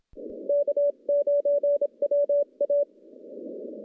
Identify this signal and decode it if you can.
Morse code, heard as a keyed tone, about 27 wpm, 580 Hz. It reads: K9WA
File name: contests / CW-SS-k5tr- -0700